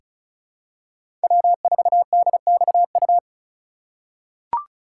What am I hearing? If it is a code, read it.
Morse code, heard as a keyed tone, about 36 wpm, 700 Hz. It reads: W4DXU
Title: USA Call Signs - 5 Characters - 2x Character Spacing - 35wpm